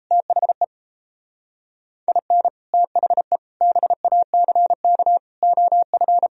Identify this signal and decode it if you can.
Morse code, heard as a keyed tone, about 35 wpm, 700 Hz. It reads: THE IN THE BACK OF
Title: Speed-Racing - Sets of 5 Words 28wpm